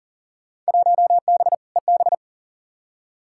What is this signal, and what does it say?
Morse code, heard as a keyed tone, about 38 wpm, 700 Hz. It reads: 1B EB
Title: Field Day Exchanges - 40wpm